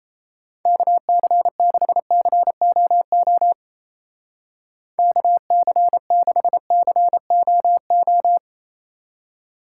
Morse code, heard as a keyed tone, about 35 wpm, 700 Hz. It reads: KC6COO KC6COO
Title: Speed Racing - USA 2x3 Call Signs 28wpm